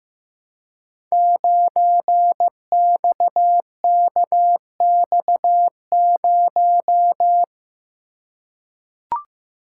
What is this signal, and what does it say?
Morse code, heard as a keyed tone, about 15 wpm, 700 Hz. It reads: 9XKX0
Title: Code Groups - 5 Letter:Numbers - 15wpm